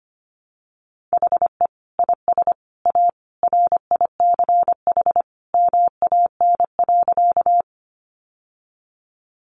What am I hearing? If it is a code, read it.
Morse code, heard as a keyed tone, about 25 wpm, 700 Hz. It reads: HE IS A RICH MAN.
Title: Sentences from Top 1100 Words 25wpm